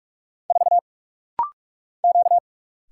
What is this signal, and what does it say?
Morse code, heard as a keyed tone, about 44 wpm, 700 Hz. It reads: 4 Q
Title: Single Letter-Number Alt - Rapid-Fire 45wpm